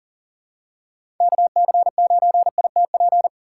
Morse code, heard as a keyed tone, about 40 wpm, 700 Hz. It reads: KC9ITP
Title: USA Call Signs - 6 Characters - 2x Character Spacing - 40wpm